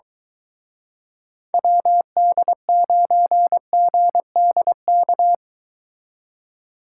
Morse code, heard as a keyed tone, about 23 wpm, 700 Hz. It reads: WD9GDK